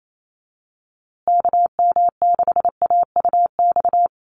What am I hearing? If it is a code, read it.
Morse code, heard as a keyed tone, about 28 wpm, 700 Hz. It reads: KM6AUX
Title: USA Call Signs Repeated 2x 28wpm